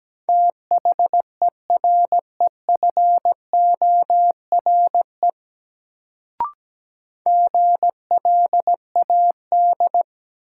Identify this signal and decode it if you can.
Morse code, heard as a keyed tone, about 17 wpm, 700 Hz. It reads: THEREFORE GLAD
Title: Sentences from Top 800 Words - New Word Drill - 17wpm